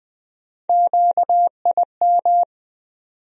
Morse code, heard as a keyed tone, about 20 wpm, 700 Hz. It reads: QIM
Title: USA Call Sign Postfixes - 3 Letters 20wpm